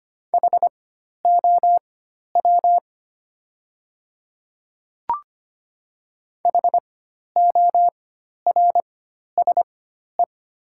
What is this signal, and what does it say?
Morse code, heard as a keyed tone, about 25 wpm, spaced as for 12 wpm, 690 Hz. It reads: HOW HORSE
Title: 4x Character Spacing - Top 300 Words - 25wpm